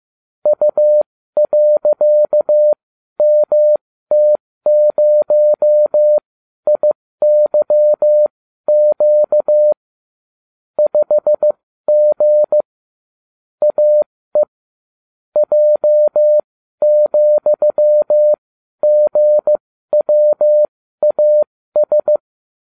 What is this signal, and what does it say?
Morse code, heard as a keyed tone, about 15 wpm, spaced as for 12 wpm, 600 Hz. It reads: U.MT0IYQ 5G AE J,GWAS